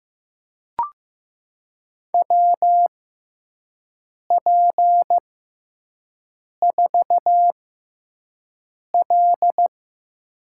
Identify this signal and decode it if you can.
Morse code, heard as a keyed tone, about 15 wpm, spaced as for 5 wpm, 700 Hz. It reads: WP4L
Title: USA Call Signs - 4 Characters - 6x Character Spacing - 15wpm